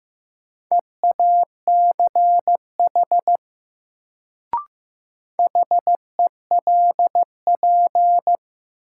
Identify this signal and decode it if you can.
Morse code, heard as a keyed tone, about 15 wpm, 700 Hz. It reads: EACH HELP
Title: Sentences from Top 200 Words - New Word Drill - 15wpm